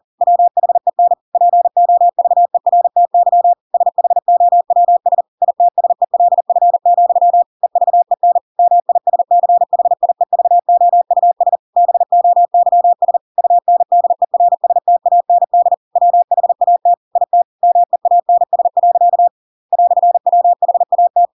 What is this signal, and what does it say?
Morse code, heard as a keyed tone, about 40 wpm, 700 Hz. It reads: WHEN POVERTY SHOWS ITSELF, EVEN MISCHIEVOUS BOYS UNDERSTAND WHAT IT MEANS. "WHAT
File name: Pinocchio / Pinocchio ch8 40 40 ik1hge